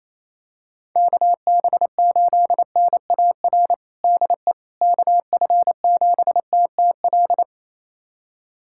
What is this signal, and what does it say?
Morse code, heard as a keyed tone, about 28 wpm, 700 Hz. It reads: KB8NAR DE KF7TTL
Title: USA Callsign DE Callsign - 6 Characters - 28wpm